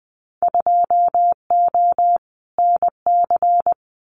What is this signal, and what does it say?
Morse code, heard as a keyed tone, about 20 wpm, 700 Hz. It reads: 2O NC
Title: Winter Field Day Exchanges 20wpm